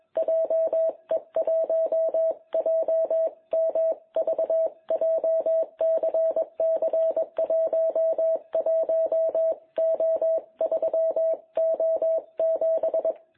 Morse code, heard as a keyed tone, about 21 wpm, 660 Hz. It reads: JE1JMV JCC11O3O7
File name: jh8cbh / 2021cw